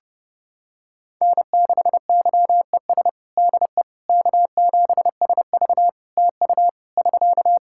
Morse code, heard as a keyed tone, about 30 wpm, 700 Hz. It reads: N6YES DE K7SV TU <SK>
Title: K1USN SST QSO Practice 30wpm